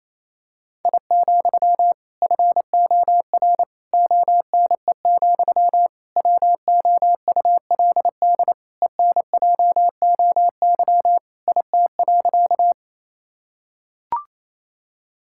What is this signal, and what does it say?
Morse code, heard as a keyed tone, about 27 wpm, 700 Hz. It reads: I, FOR ONE, WOULD ENJOY IT.
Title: Sentences from Top 1000 Words - Review Part 5 - 28wpm